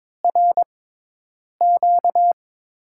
Morse code, heard as a keyed tone, about 21 wpm, spaced as for 8 wpm, 700 Hz. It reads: RQ
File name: Top 90 QSO Elements - 6x Character Spacing - 22wpm